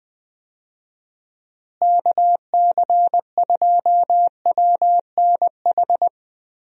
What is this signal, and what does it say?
Morse code, heard as a keyed tone, about 21 wpm, 700 Hz. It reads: KC2WNH
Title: USA Call Signs - 6 Characters - Repeated 3x - 20wpm